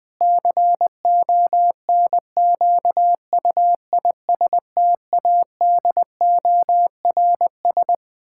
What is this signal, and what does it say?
Morse code, heard as a keyed tone, about 21 wpm, 700 Hz. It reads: CONQUISTADORS
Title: Pangram Challenge v2 - 20wpm